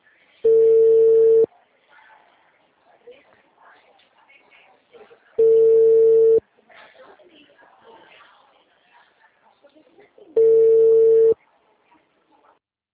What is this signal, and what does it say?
Ringback tone (China, 450 Hz), with speech.
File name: out- -2012- - - .70429